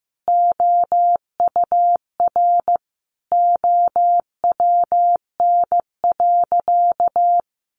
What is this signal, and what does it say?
Morse code, heard as a keyed tone, about 15 wpm, 700 Hz. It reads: OUR OWN.
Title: Sentences from Top 1500 Words 15wpm